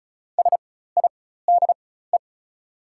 Morse code, heard as a keyed tone, about 36 wpm, spaced as for 16 wpm, 690 Hz. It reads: SIDE
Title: 4x Character Spacing - Top 100 Words - 35wpm